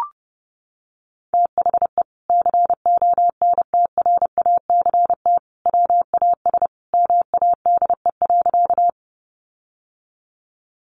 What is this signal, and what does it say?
Morse code, heard as a keyed tone, about 30 wpm, 700 Hz. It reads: THE CONTRACT WAS MADE.
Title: Sentences from Top 1400 Words 30wpm